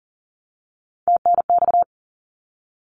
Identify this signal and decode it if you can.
Morse code, heard as a keyed tone, about 40 wpm, 700 Hz. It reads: TNX
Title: Abbreviations Repeated 3x - 40wpm